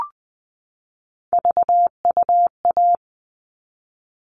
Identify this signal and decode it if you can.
Morse code, heard as a keyed tone, about 20 wpm, 700 Hz. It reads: VUA
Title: All Cut Numbers - 3-Digit 20wpm